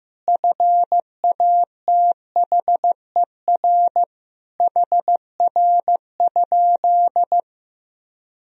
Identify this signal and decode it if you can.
Morse code, heard as a keyed tone, about 16 wpm, 700 Hz. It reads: FATHER HR?
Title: Sentences from Top 300 Words and Abbreviations 15wpm